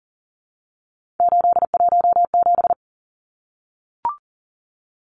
Morse code, heard as a keyed tone, about 40 wpm, 700 Hz. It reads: 817
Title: Numbers - 3-Digit 40wpm